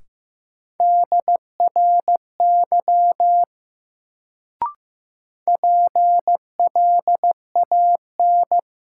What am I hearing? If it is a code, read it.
Morse code, heard as a keyed tone, about 16 wpm, 700 Hz. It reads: DRY PLAN
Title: Sentences from Top 500 Words - New Word Drill - 15wpm